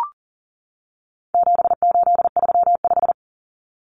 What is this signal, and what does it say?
Morse code, heard as a keyed tone, about 40 wpm, 700 Hz. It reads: 7835
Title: Numbers - 4-Digit 40wpm